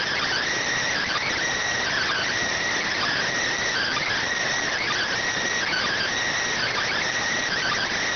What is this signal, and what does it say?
Emergency Alert System (SAME) tones (header bursts), heard as harsh buzzing bursts. It is heard from 0.7 to 7.2 s.